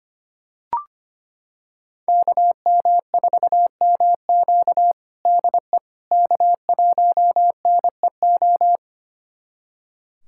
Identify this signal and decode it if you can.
Morse code, heard as a keyed tone, about 25 wpm, 700 Hz. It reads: KM4MQ DE K1NEO
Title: USA Callsign DE Callsign - 5 Characters - 25wpm